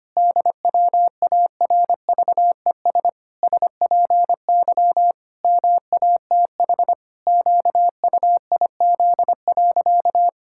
Morse code, heard as a keyed tone, about 25 wpm, 700 Hz. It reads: DWARVES SPY MATH QUIZ.
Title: Pangram Challenge v2 - 25wpm